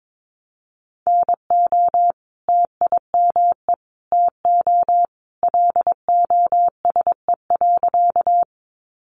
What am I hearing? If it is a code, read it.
Morse code, heard as a keyed tone, about 22 wpm, 700 Hz. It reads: NO TIME TO LOSE.
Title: Sentences from Top 1100 Words 22wpm